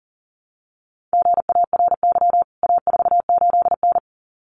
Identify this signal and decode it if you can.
Morse code, heard as a keyed tone, about 40 wpm, 700 Hz. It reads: GARY A48N
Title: CWT Exchange 40wpm